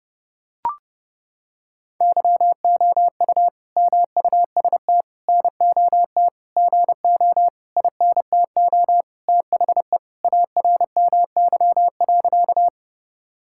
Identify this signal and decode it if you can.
Morse code, heard as a keyed tone, about 28 wpm, 700 Hz. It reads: YOU MUST NOT GO INTO THE ARMY.
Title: Sentences from Top 1000 Words - Review Part 5 - 30wpm